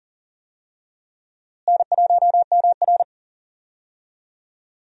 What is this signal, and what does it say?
Morse code, heard as a keyed tone, about 40 wpm, 700 Hz. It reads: N1MR
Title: USA Call Signs - 4 Characters - Repeated 2x - 40wpm